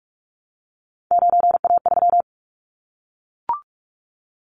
Morse code, heard as a keyed tone, about 44 wpm, 700 Hz. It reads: 9A3
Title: Callsign Prefixes for Top 100 DXCC - 45wpm